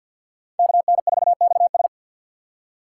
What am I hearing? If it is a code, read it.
Morse code, heard as a keyed tone, about 47 wpm, 700 Hz. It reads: KN4XS
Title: USA Call Signs - 5 Characters - 6x Character Spacing - 50wpm